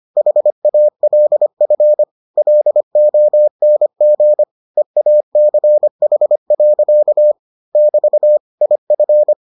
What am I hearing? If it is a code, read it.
Morse code, heard as a keyed tone, about 25 wpm, 600 Hz. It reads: HALF LONG EACH. <BT> IF